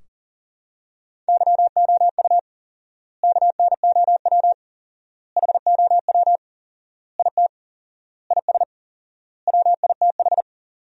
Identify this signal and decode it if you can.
Morse code, heard as a keyed tone, about 40 wpm, 700 Hz. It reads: YOU KNOW HOW IT IS WITH
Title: 4x Word-Spacing - Sentences from Top 100 Words - 40wpm